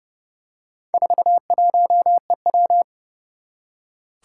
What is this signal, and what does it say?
Morse code, heard as a keyed tone, about 28 wpm, 690 Hz. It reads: 41EW
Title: Four Challenging Letter-Number Combinations 30wpm